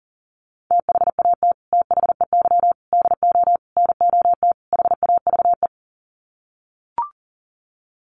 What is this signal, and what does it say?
Morse code, heard as a keyed tone, about 40 wpm, 700 Hz. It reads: THAT THEY DO NOT HAVE
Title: Sets of 5 Words Encore2 40wpm